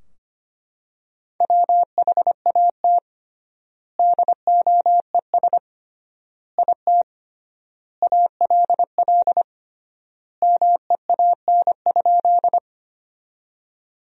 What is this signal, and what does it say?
Morse code, heard as a keyed tone, about 25 wpm, 700 Hz. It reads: WHAT DOES IT ALL MEAN?
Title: 3x Word-Spacing - Sentences from Top 200 Words - 25wpm